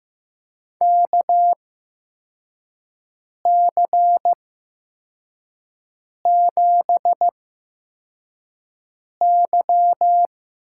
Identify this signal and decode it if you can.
Morse code, heard as a keyed tone, about 16 wpm, spaced as for 4 wpm, 700 Hz. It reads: KC7Y
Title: USA Call Signs - 6 Characters - 8x Character Spacing - 15wpm